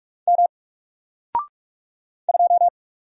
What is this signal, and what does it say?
Morse code, heard as a keyed tone, about 45 wpm, 700 Hz. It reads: M 2
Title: Single Letter-Number 45wpm